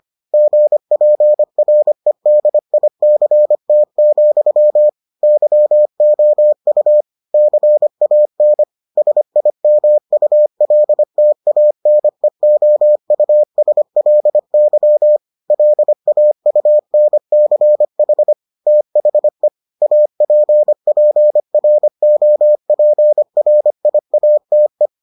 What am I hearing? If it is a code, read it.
Morse code, heard as a keyed tone, about 25 wpm, 600 Hz. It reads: GPREDICT, YOU CAN SIMULTANEOUSLY LAUNCH THE APPROPRIATE